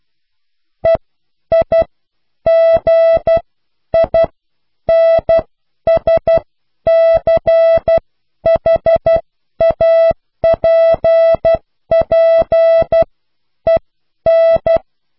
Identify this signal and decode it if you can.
Morse code, heard as a keyed tone, about 12 wpm, spaced as for 9 wpm, 660 Hz. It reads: EIGINSCHAPPEN